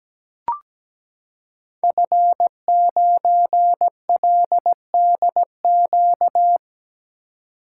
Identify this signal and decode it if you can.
Morse code, heard as a keyed tone, about 17 wpm, 700 Hz. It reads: F9LDQ
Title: Code Groups - 5 Letter:Numbers - 17wpm